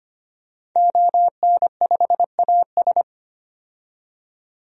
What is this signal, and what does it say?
Morse code, heard as a keyed tone, about 25 wpm, 700 Hz. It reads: ON5AS
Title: DXCC-100 Call Signs - 5 Characters - 25wpm